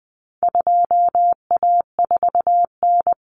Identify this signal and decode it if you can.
Morse code, heard as a keyed tone, about 20 wpm, 700 Hz. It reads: 2A4N
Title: Common Cut Numbers - 4-Digit 20wpm